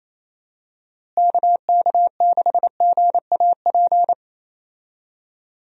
Morse code, heard as a keyed tone, about 28 wpm, 700 Hz. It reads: KK6GAP
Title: USA Call Signs - 6 Characters - Repeated 3x - 28wpm